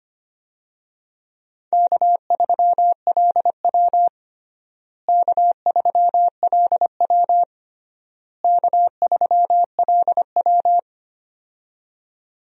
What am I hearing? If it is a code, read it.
Morse code, heard as a keyed tone, about 25 wpm, 700 Hz. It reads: K3LW K3LW K3LW
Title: USA Call Signs - 4 Characters - Repeated 3x - 25wpm